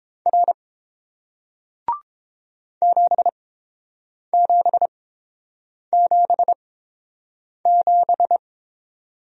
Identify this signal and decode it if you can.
Morse code, heard as a keyed tone, about 30 wpm, spaced as for 7 wpm, 700 Hz. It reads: R 7777